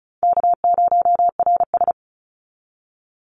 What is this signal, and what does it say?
Morse code, heard as a keyed tone, about 36 wpm, 700 Hz. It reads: K0RS
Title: NA SOTA Chaser Callsigns 35wpm